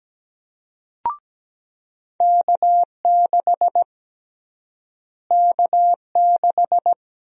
Morse code, heard as a keyed tone, about 17 wpm, 700 Hz. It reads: K6 K6
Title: USA Call Sign Prefixes - Repeated 2x - 17wpm